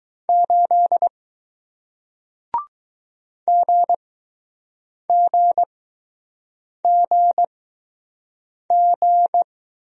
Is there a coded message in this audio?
Yes: Morse code from 0.3 to 9.4 s, heard as a keyed tone, about 20 wpm, spaced as for 6 wpm, 700 Hz. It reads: 8 GGGG